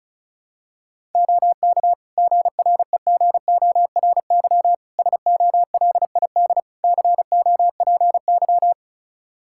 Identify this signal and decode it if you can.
Morse code, heard as a keyed tone, about 36 wpm, 700 Hz. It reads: OK GREGORY SOLID COPY
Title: Practice QSO - Exchange 3 - More Info - 35wpm